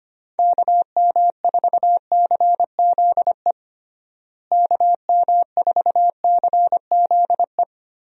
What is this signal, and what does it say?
Morse code, heard as a keyed tone, about 25 wpm, 700 Hz. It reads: KM4CZE KM4CZE
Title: USA Call Signs - 6 Characters - Repeated 3x - 25wpm